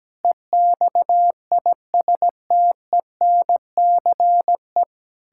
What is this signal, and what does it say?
Morse code, heard as a keyed tone, about 17 wpm, 700 Hz. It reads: EXISTENCE
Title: Pangram Challenge v2 - 17wpm